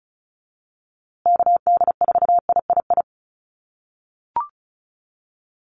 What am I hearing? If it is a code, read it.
Morse code, heard as a keyed tone, about 34 wpm, 700 Hz. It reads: KD4III
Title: USA Call Signs Repeated 2x 35wpm